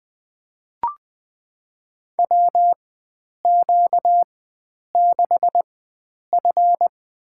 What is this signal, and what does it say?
Morse code, heard as a keyed tone, about 20 wpm, spaced as for 9 wpm, 700 Hz. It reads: WQ6F
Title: USA Call Signs - 4 Characters - 4x Character Spacing - 20wpm